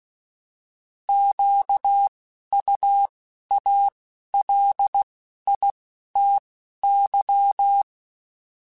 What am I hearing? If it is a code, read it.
Morse code, heard as a keyed tone, about 16 wpm, 800 Hz. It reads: Q U A L I T Y
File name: Misc / Misc-mp3s / Misc @10